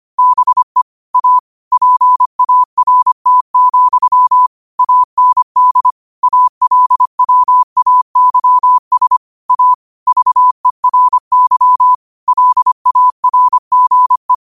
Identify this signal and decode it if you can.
Morse code, heard as a keyed tone, about 25 wpm, 1000 Hz. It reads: BE A PART, AND ALWAYS A VERY LARGE